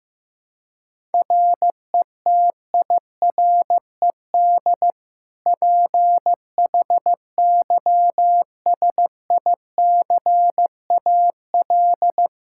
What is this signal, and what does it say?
Morse code, heard as a keyed tone, about 15 wpm, 700 Hz. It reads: RETIRED PHYSICAL
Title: Practice QSO - Exchange 3 - More Info - 15wpm